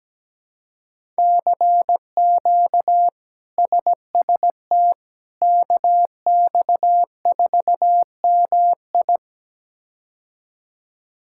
Morse code, heard as a keyed tone, about 17 wpm, 700 Hz. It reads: CQ SST KX4MI
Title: K1USN SST QSO Practice - 17wpm